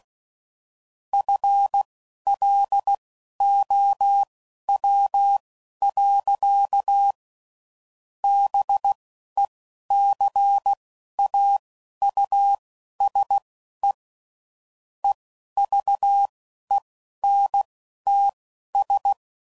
Morse code, heard as a keyed tone, about 16 wpm, spaced as for 11 wpm, 800 Hz. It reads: FLOW. BECAUSE EVENTS
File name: CW Code Groups @10WPM / Technical Subjects / XL